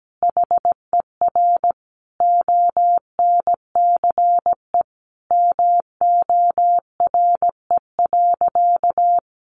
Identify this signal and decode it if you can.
Morse code, heard as a keyed tone, about 17 wpm, 700 Hz. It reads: HER ONCE MORE.
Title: Sentences from Top 500 Words 17wpm